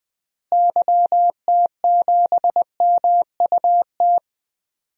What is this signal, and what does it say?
Morse code, heard as a keyed tone, about 20 wpm, 700 Hz. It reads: YT7MUT
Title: DXCC-100 Call Signs - 6 Characters - Repeated 3x - 20wpm